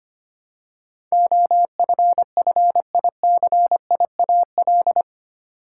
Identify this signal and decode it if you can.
Morse code, heard as a keyed tone, about 25 wpm, 700 Hz. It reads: OFFICIAL
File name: Eight Letter Words - 25wpm